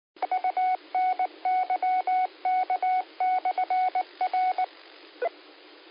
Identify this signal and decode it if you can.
Morse code, heard as a keyed tone, about 19 wpm, 740 Hz. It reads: UNYK/R
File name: qsorec SimplexLogic